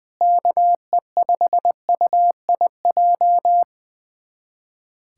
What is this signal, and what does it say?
Morse code, heard as a keyed tone, about 21 wpm, 700 Hz. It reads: KE5UIJ
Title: USA Callsign DE Callsign 20wpm